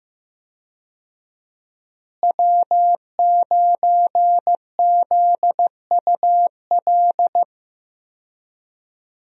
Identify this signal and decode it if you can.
Morse code, heard as a keyed tone, about 15 wpm, 700 Hz. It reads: W9ZUL
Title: USA Call Signs - 5 Characters - Repeated 3x - 15wpm